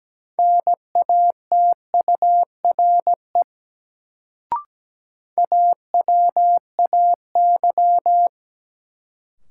Morse code, heard as a keyed tone, about 17 wpm, 700 Hz. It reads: NATURE AWAY
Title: Sentences from Top 600 Words - New Word Drill - 17wpm